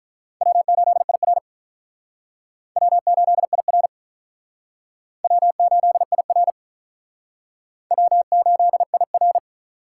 Morse code, heard as a keyed tone, about 44 wpm, 700 Hz. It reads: W8IR W8IR W8IR W8IR
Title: Speed Racing - USA 1x2 Call Signs 35wpm